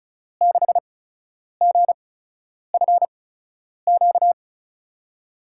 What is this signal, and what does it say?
Morse code, heard as a keyed tone, about 36 wpm, spaced as for 10 wpm, 700 Hz. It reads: 6GFQ
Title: USA Call Signs - 6 Characters - 8x Character Spacing - 35wpm